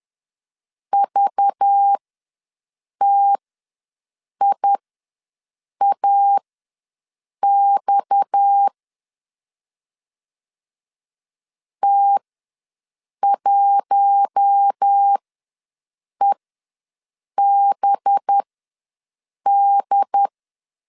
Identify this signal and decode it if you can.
Morse code, heard as a keyed tone, about 10 wpm, spaced as for 6 wpm, 800 Hz. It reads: VTIAX T1EBD
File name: LEZIONI / 8 YJ45 / 8 YJ ws 10wpm 4043